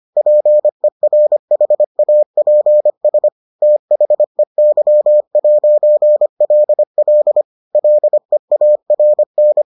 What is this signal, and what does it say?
Morse code, heard as a keyed tone, about 25 wpm, 600 Hz. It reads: PERHAPS THEY'LL LEARN